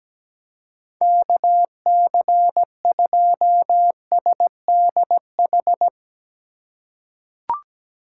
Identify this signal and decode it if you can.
Morse code, heard as a keyed tone, about 17 wpm, 700 Hz. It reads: KC2SDH
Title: USA Call Signs - 6 Characters - 4x Character Spacing - 17wpm